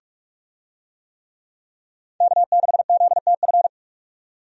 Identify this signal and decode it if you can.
Morse code, heard as a keyed tone, about 42 wpm, 700 Hz. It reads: K6ZTF
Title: USA Call Signs - 5 Characters - 45wpm